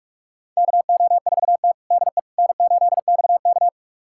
Morse code, heard as a keyed tone, about 41 wpm, 700 Hz. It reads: KO4T DE N8XK
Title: USA Callsign DE Callsign - 4 Characters - 45wpm